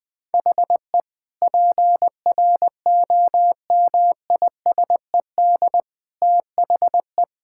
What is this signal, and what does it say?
Morse code, heard as a keyed tone, about 20 wpm, 700 Hz. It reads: HE PROMISED THE
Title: Sentences with 8-Word Phrases - 20wpm